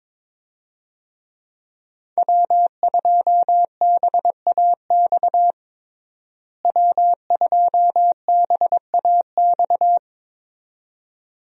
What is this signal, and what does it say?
Morse code, heard as a keyed tone, about 21 wpm, 700 Hz. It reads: W2BAX W2BAX
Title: USA Call Signs - 5 Characters - Repeated 2x - 22wpm